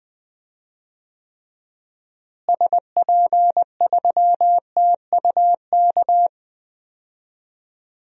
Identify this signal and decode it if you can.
Morse code, heard as a keyed tone, about 21 wpm, 700 Hz. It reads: SP3TUK
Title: DXCC-100 Call Signs - 6 Characters - Repeated 3x - 20wpm